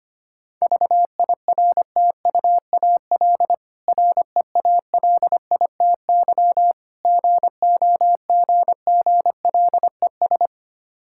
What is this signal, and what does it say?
Morse code, heard as a keyed tone, about 25 wpm, 690 Hz. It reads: VIRTUAL REALITY GOGGLES
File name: practice / 25wpm / Difficult Words with Hints - 25wpm